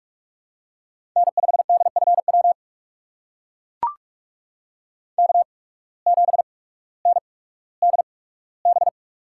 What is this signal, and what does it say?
Morse code, heard as a keyed tone, about 44 wpm, spaced as for 19 wpm, 690 Hz. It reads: N5DFW K7NDB
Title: USA Call Signs - 5 Characters - 8x Character Spacing - 45wpm